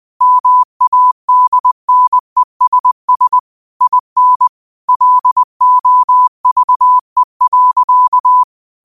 Morse code, heard as a keyed tone, about 20 wpm, 1000 Hz. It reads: MADNESS IN LOVE.
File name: qotdfiles / 20wpm